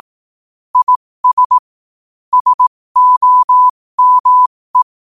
Morse code, heard as a keyed tone, about 18 wpm, spaced as for 15 wpm, 1000 Hz. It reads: IS SOME
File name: qotdfiles / 15wpm